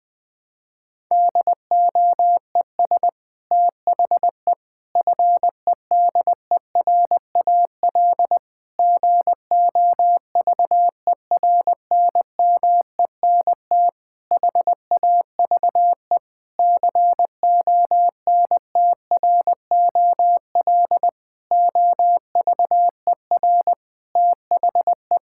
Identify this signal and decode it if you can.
Morse code, heard as a keyed tone, about 20 wpm, 700 Hz. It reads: DOES THE FEDERAL GOVERNMENT HAVE CONTROL OVER THE
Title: Sentences from Top 1000 Words - Review Part 5 - 20wpm